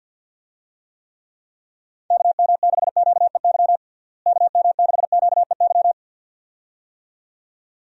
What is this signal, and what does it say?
Morse code, heard as a keyed tone, about 50 wpm, 700 Hz. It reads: KM6QEY KM6QEY
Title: USA Call Signs - 6 Characters - Repeated 2x - 50wpm